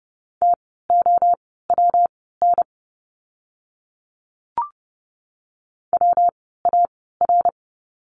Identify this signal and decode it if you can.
Morse code, heard as a keyed tone, about 30 wpm, spaced as for 17 wpm, 700 Hz. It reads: TOWN WAR
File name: course / 30:10wpm / 115 Lesson 19f - Letter w - Words 30wpm